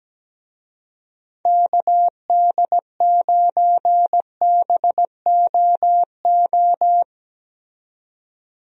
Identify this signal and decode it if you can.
Morse code, heard as a keyed tone, about 17 wpm, 700 Hz. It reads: KD9BOO
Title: USA Call Signs - 6 Characters - 17wpm